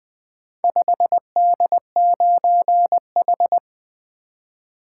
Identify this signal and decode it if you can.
Morse code, heard as a keyed tone, about 20 wpm, 700 Hz. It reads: 5D9H
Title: Four Letter-Number Combinations - 20wpm